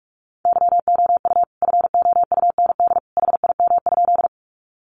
Morse code, heard as a keyed tone, about 44 wpm, 700 Hz. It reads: YOU FOUND HIM?
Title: Sentences from Top 300 Words 45wpm